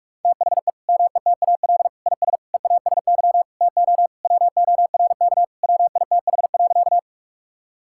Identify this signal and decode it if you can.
Morse code, heard as a keyed tone, about 46 wpm, 700 Hz. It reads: THE METAL IS EASY TO WORK WITH.
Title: Sentences from Top 1000 Words - Review Part 6 - 45wpm